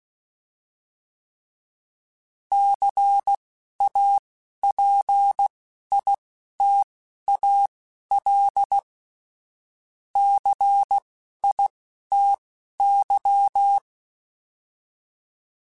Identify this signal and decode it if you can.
Morse code, heard as a keyed tone, about 16 wpm, spaced as for 11 wpm, 800 Hz. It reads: CAPITAL CITY